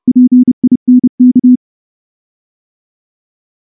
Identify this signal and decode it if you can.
Morse code, heard as a keyed tone, about 28 wpm, 250 Hz. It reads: PINK
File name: morse / morse-training-30WPM